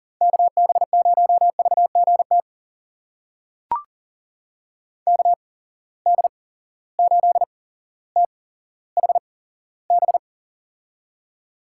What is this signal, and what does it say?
Morse code, heard as a keyed tone, about 40 wpm, spaced as for 17 wpm, 700 Hz. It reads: KB0VGT KD8THB
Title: USA Call Signs - 6 Characters - 8x Character Spacing - 40wpm